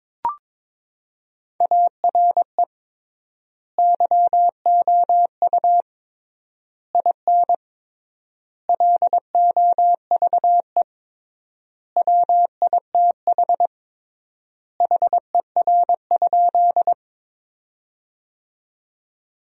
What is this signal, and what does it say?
Morse code, heard as a keyed tone, about 22 wpm, 700 Hz. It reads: ARE YOU IN LOVE WITH HER?
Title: 3x Word-Spacing - Sentences from Top 400 Words - 22wpm